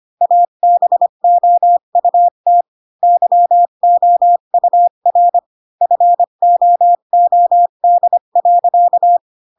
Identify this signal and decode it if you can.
Morse code, heard as a keyed tone, about 25 wpm, 700 Hz. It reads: ABOUT YOUR FOOD.
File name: Pinocchio / Pinocchio ch7 20 25 ik1hge